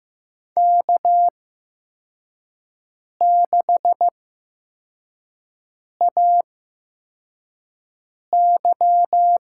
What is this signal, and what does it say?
Morse code, heard as a keyed tone, about 15 wpm, spaced as for 4 wpm, 700 Hz. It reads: K6AY
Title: USA Call Signs - 8x Character Spacing - 15wpm